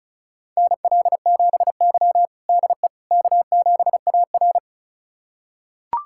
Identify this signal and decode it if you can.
Morse code, heard as a keyed tone, about 36 wpm, 700 Hz. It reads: NL7Y DE K7AR
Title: USA Callsign DE Callsign - 4 Characters - 35wpm